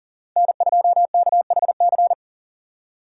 Morse code, heard as a keyed tone, about 40 wpm, 700 Hz. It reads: N2KHC
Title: USA Call Signs - 2x Character Spacing - 40wpm